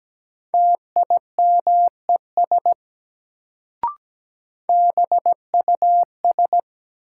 Morse code, heard as a keyed tone, about 17 wpm, 700 Hz. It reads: TIMES BUS
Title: Sentences from Top 1000 Words - Review Part 6 - New Word Drill - 17wpm